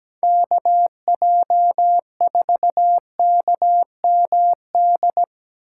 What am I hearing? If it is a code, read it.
Morse code, heard as a keyed tone, about 17 wpm, 700 Hz. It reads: KJ4KMD
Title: USA Call Signs - 6 Characters - 6x Character Spacing - 17wpm